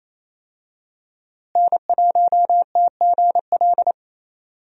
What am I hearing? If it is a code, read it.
Morse code, heard as a keyed tone, about 27 wpm, 700 Hz. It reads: N1TGL
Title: USA Call Signs - 5 Characters - Repeated 2x - 28wpm